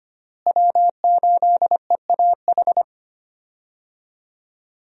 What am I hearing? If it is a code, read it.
Morse code, heard as a keyed tone, about 25 wpm, 700 Hz. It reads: W8EAH
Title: USA Call Signs - 5 Characters - Repeated 2x - 25wpm